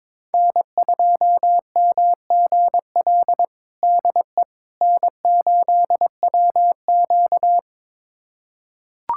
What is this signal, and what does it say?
Morse code, heard as a keyed tone, about 22 wpm, 700 Hz. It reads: N2MGL DE N8WQ
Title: USA Callsign DE Callsign 22wpm